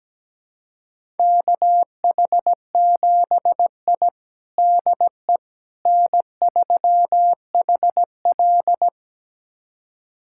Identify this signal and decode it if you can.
Morse code, heard as a keyed tone, about 17 wpm, 700 Hz. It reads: KH7I DE N3HL
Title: USA Callsign DE Callsign - 4 Characters - 17wpm